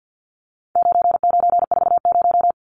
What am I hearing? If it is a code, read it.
Morse code, heard as a keyed tone, about 50 wpm, 700 Hz. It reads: 9940
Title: Numbers - 4-Digit 50wpm